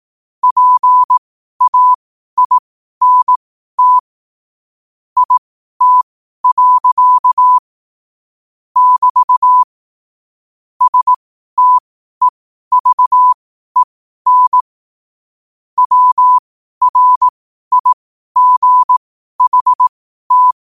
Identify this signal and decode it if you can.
Morse code, heard as a keyed tone, about 18 wpm, spaced as for 12 wpm, 1000 Hz. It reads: PAINT IT. <BT> STEVEN WRIGHT
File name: qotdfiles / 12wpm